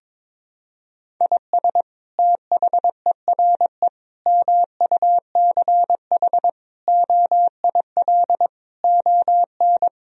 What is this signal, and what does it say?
Morse code, heard as a keyed tone, about 22 wpm, 700 Hz. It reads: IS THERE MUCH OIL ON
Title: Sentences from Top 1000 Words - Review Part 4 - 22wpm